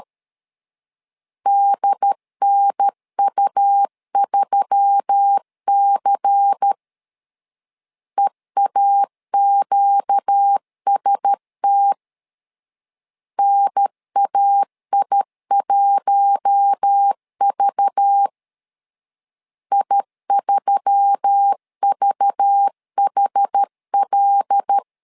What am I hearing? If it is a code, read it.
Morse code, heard as a keyed tone, about 13 wpm, 800 Hz. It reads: DNU3C EAQST NAI1V I3VHL
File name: LEZIONI / 5 LBVC / 5 LBVC 12wpm 1955